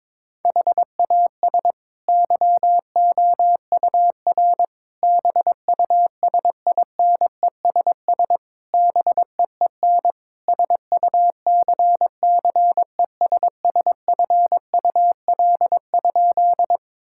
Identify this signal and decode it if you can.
Morse code, heard as a keyed tone, about 22 wpm, 700 Hz. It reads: HAS YOUR BUSINESS BEEN SUCCESSFUL?
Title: Sentences from Top 1000 Words - Review Part 6 - 22wpm